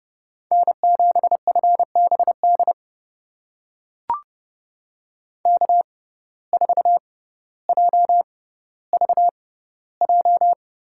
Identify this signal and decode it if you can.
Morse code, heard as a keyed tone, about 30 wpm, spaced as for 15 wpm, 700 Hz. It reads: N7FBD K4JVJ
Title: USA Call Signs - 6x Character Spacing - 30wpm